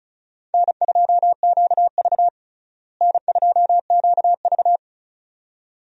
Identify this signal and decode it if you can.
Morse code, heard as a keyed tone, about 36 wpm, 700 Hz. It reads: N2QV N2QV
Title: USA Call Signs - 4 Characters - Repeated 2x - 35wpm